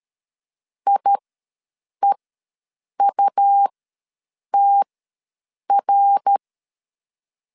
Morse code, heard as a keyed tone, about 13 wpm, spaced as for 7 wpm, 800 Hz. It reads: IEUTR
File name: LEZIONI / 9 28 ws 12wpm 3354